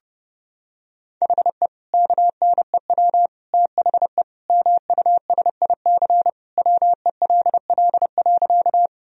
Morse code, heard as a keyed tone, about 31 wpm, 700 Hz. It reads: HE KNEW THE MUSIC WELL.
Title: Sentences from Top 1000 Words - Review Part 1 - 30wpm